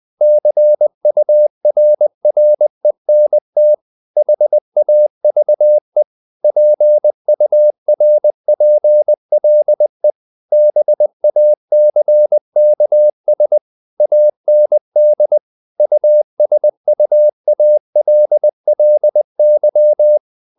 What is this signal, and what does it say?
Morse code, heard as a keyed tone, about 20 wpm, 600 Hz. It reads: CURRENT HAVE PURPLE BACKS AND USUALLY
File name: the old man and the sea part 2 20wpm